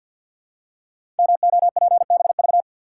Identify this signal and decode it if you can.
Morse code, heard as a keyed tone, about 51 wpm, 700 Hz. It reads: MOPBV
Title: Code Groups - 5 Letters - 50wpm